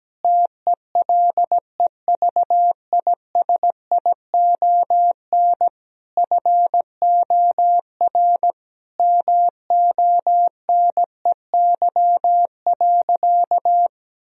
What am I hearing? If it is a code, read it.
Morse code, heard as a keyed tone, about 17 wpm, 700 Hz. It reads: TELEVISION FOR MONEY.
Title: Sentences from Top 1000 Words - Review Part 4 - 17wpm